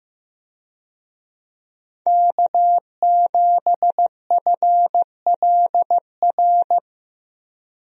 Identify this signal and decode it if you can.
Morse code, heard as a keyed tone, about 15 wpm, 700 Hz. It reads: K7FLR
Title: USA Call Signs - 5 Characters - Repeated 3x - 15wpm